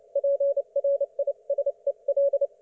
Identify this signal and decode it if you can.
Morse code, heard as a keyed tone, about 29 wpm, 560 Hz. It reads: PRISEL